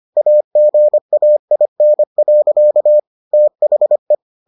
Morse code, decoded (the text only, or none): AGAIN. THE